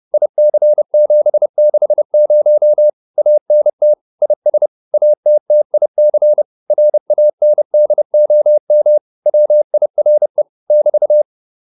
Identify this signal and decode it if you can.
Morse code, heard as a keyed tone, about 30 wpm, 600 Hz. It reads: IC760 ANT IS ATTIC RANDOM WIRE <BT>
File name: QSO206 30